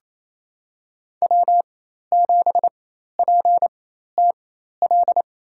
Morse code, heard as a keyed tone, about 28 wpm, spaced as for 13 wpm, 700 Hz. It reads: W7PTL
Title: USA Call Signs - 5 Characters - 4x Character Spacing - 28wpm